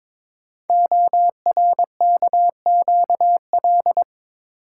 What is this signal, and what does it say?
Morse code, heard as a keyed tone, about 22 wpm, 700 Hz. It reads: ORKQL
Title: Code Groups - 5 Letters - 22wpm